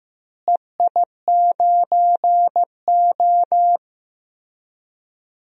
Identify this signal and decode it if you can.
Morse code, heard as a keyed tone, about 15 wpm, 700 Hz. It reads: EI9O
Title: DXCC-100 Call Signs - 4 Characters - Repeated 3x - 15wpm